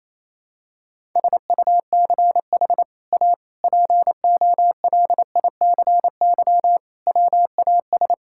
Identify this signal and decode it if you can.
Morse code, heard as a keyed tone, about 28 wpm, 700 Hz. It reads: SUCH A POLICY WAS
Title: Sentences from Top 1000 Words - Review Part 5 - 28wpm